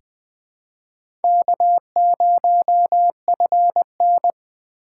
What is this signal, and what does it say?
Morse code, heard as a keyed tone, about 20 wpm, 700 Hz. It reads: K0FN
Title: USA Call Signs - 4 Characters - 6x Character Spacing - 20wpm